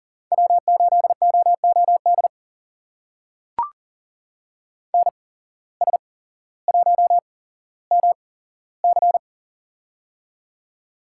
Morse code, heard as a keyed tone, about 38 wpm, spaced as for 17 wpm, 700 Hz. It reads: W8OOD NS1MC
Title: USA Call Signs - 5 Characters - 8x Character Spacing - 40wpm